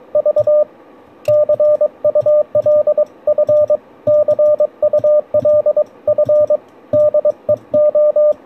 Morse code, heard as a keyed tone, about 22 wpm, 590 Hz, with radio noise. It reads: V CULF CULF DEO